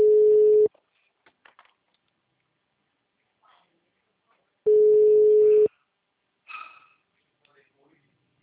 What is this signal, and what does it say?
Ringback tone (Europe, 425 Hz), with speech.